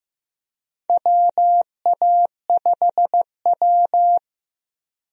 Morse code, heard as a keyed tone, about 15 wpm, 700 Hz. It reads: WA5W
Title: USA Call Signs - 4 Characters - Repeated 2x - 15wpm